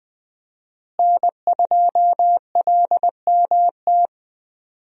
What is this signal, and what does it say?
Morse code, heard as a keyed tone, about 20 wpm, 700 Hz. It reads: N2LMT
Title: USA Call Signs - 5 Characters - Repeated 3x - 20wpm